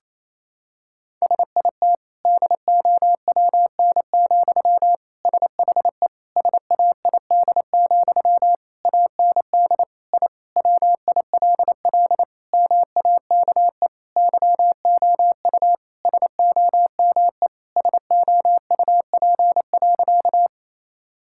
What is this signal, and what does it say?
Morse code, heard as a keyed tone, about 28 wpm, 700 Hz. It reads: SIT DOWN, SHE SAID, AND I WILL MAKE YOU SOME SOUP.
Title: Sentences from Top 1000 Words - Review Part 6 - 28wpm